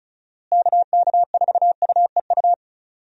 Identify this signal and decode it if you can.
Morse code, heard as a keyed tone, about 36 wpm, 700 Hz. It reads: KK4UEU
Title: USA Call Signs - 6 Characters - Repeated 2x - 35wpm